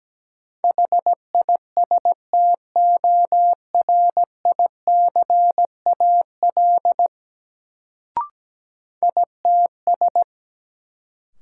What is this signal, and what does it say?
Morse code, heard as a keyed tone, about 17 wpm, 700 Hz. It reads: HISTORICAL ITS
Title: Sentences from Top 600 Words - New Word Drill - 17wpm